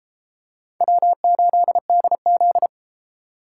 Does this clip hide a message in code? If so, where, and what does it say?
Yes: Morse code from 0.8 to 2.7 s, heard as a keyed tone, about 33 wpm, 700 Hz. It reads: W8DZ